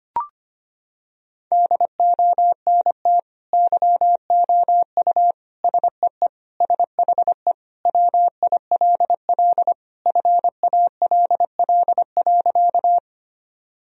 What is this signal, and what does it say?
Morse code, heard as a keyed tone, about 25 wpm, 700 Hz. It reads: DONT YOU SEE SHE WILL FALL.
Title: Sentences from Top 1000 Words - Review Part 2 - 25wpm